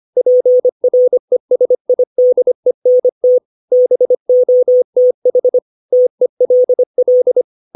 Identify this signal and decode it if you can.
Morse code, heard as a keyed tone, about 25 wpm, 500 Hz. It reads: PRESIDENT BOTH TELL
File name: 632 words / 632 Words 25WPM